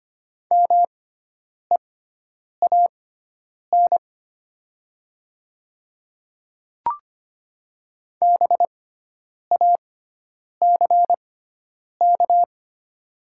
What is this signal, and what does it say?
Morse code, heard as a keyed tone, about 25 wpm, spaced as for 9 wpm, 700 Hz. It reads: MEAN BACK
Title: 6x Character Spacing - Top 500 Words - 25wpm